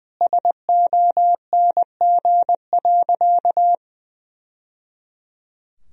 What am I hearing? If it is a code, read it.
Morse code, heard as a keyed tone, about 20 wpm, 700 Hz. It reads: SONG.
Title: Sentences from Top 1000 Words - Review Part 2 - 20wpm